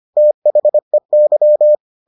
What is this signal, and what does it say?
Morse code, heard as a keyed tone, about 25 wpm, 600 Hz. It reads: THEY